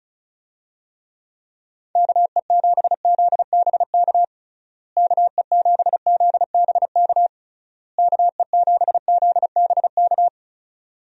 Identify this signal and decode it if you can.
Morse code, heard as a keyed tone, about 34 wpm, 700 Hz. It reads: KE7ZBK KE7ZBK KE7ZBK
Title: USA Call Signs - 6 Characters - Repeated 3x - 35wpm